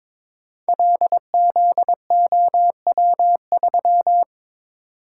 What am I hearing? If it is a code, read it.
Morse code, heard as a keyed tone, about 22 wpm, 700 Hz. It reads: LZOW3
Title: Code Groups - 5 Letter:Numbers - 22wpm